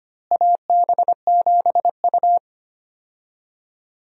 Morse code, heard as a keyed tone, about 25 wpm, 700 Hz. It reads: AB7U
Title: USA Call Signs - 4 Characters - Repeated 3x - 25wpm